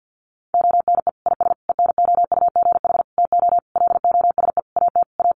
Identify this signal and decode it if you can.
Morse code, heard as a keyed tone, about 50 wpm, 700 Hz. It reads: ONE IS ENOUGH TO LOSE AT A
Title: Sentences from Top 1100 Words 50wpm